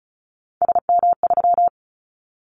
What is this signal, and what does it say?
Morse code, heard as a keyed tone, about 34 wpm, 700 Hz. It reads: SM3
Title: Callsign Prefixes for Top 25 DXCC - 35wpm